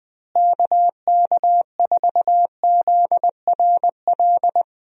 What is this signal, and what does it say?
Morse code, heard as a keyed tone, about 20 wpm, 700 Hz. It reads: KK4ZRL
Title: USA Call Signs - 6 Characters - Repeated 2x - 20wpm